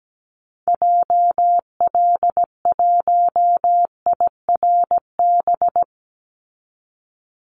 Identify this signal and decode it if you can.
Morse code, heard as a keyed tone, about 17 wpm, 700 Hz. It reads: JL1IRB
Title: Call Signs Repeated 3x - 17wpm